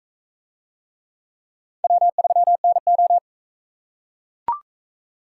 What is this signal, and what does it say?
Morse code, heard as a keyed tone, about 44 wpm, 700 Hz. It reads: W3NO